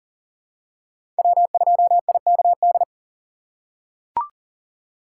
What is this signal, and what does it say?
Morse code, heard as a keyed tone, about 37 wpm, 700 Hz. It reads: W2IKD
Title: USA Call Signs - 5 Characters - 6x Character Spacing - 40wpm